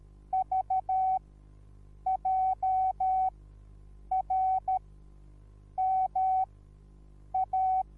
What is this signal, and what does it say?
Morse code, heard as a keyed tone, about 13 wpm, spaced as for 7 wpm, 760 Hz. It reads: VJRMA
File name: Onlinelehrgang / morsen / DJ4UF Morsen21